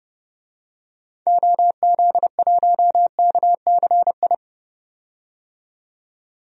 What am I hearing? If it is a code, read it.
Morse code, heard as a keyed tone, about 30 wpm, 700 Hz. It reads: OZ1KCI
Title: DXCC-100 Call Signs - 6 Characters - 30wpm